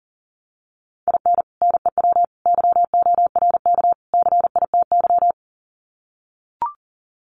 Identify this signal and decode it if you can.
Morse code, heard as a keyed tone, about 40 wpm, 700 Hz. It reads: IN NEW YORK CITY
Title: Sets of 4 Words 40wpm